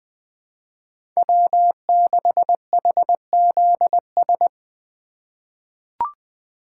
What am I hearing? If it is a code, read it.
Morse code, heard as a keyed tone, about 20 wpm, 700 Hz. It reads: W6HZS
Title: USA Call Signs - 8x Character Spacing - 20wpm